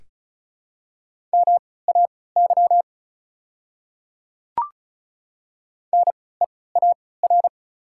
Morse code, heard as a keyed tone, about 36 wpm, spaced as for 20 wpm, 700 Hz. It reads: MAY NEAR
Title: 3x Character Spacing - Top 200 Words - 35wpm